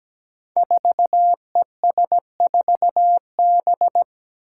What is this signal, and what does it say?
Morse code, heard as a keyed tone, about 17 wpm, 700 Hz. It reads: 4ES4B
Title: Five Challenging Letter-Number Combinations 17wpm